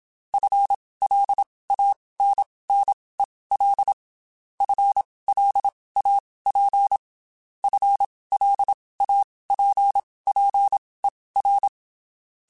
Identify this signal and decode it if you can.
Morse code, heard as a keyed tone, about 27 wpm, spaced as for 19 wpm, 800 Hz. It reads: FLANNEL FLAP FLAPPER
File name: Spaced / FL Runs sp mp3s / FL Runs sp @25